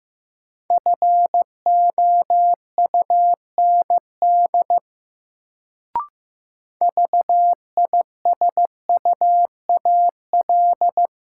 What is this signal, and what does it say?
Morse code, heard as a keyed tone, about 15 wpm, 700 Hz. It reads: FOUND VISUAL
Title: Sets of 5 Words Encore - New Word Drill - 15wpm